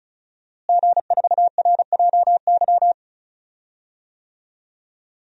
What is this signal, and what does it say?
Morse code, heard as a keyed tone, about 36 wpm, 690 Hz. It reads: G4RJY
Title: DXCC-100 Call Signs - 5 Characters - Repeated 3x - 35wpm